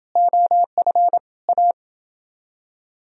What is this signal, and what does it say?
Morse code, heard as a keyed tone, about 29 wpm, 700 Hz. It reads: OF A